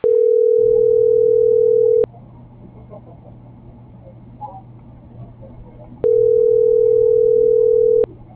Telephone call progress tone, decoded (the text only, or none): ringback tone